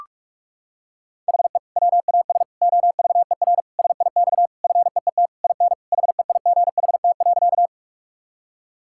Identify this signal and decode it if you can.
Morse code, heard as a keyed tone, about 46 wpm, 700 Hz. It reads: HE WAS OVER SIX FEET IN HEIGHT.
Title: Sentences from Top 1600 Words 45wpm